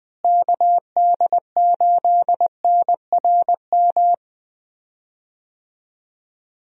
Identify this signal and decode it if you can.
Morse code, heard as a keyed tone, about 20 wpm, 700 Hz. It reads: KD8NRM
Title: USA Call Signs - 6 Characters - Repeated 2x - 20wpm